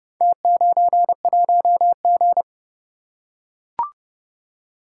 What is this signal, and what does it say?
Morse code, heard as a keyed tone, about 29 wpm, 700 Hz. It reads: T91G